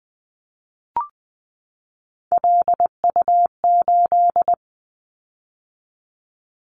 Morse code, heard as a keyed tone, about 20 wpm, 700 Hz. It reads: LU8
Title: Callsign Prefixes for Top 50 DXCC - 20wpm